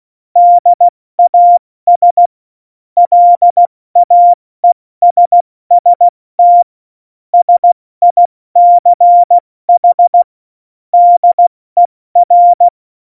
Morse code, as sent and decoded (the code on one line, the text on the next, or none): -.. .- ... / .-.. .- . ... ... - / ... .. -.-. .... / -.. .-.
DAS LAESST SICH DER